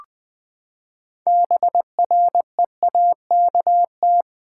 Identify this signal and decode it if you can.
Morse code, heard as a keyed tone, about 20 wpm, 700 Hz. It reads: BREAKT
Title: Build A Word - 6 Letters At A Time - 12-Letter Words - 20wpm